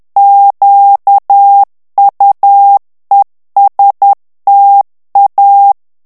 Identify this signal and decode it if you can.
Morse code, heard as a keyed tone, about 10 wpm, 800 Hz. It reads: QUESTA
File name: morse mp3 / ITALIEN texte clair 1 vitesse 10 mots minute